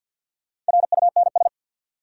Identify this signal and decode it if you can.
Morse code, heard as a keyed tone, about 50 wpm, 700 Hz. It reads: RUNS
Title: Sentences from Top 600 Words - New Word Drill - 50wpm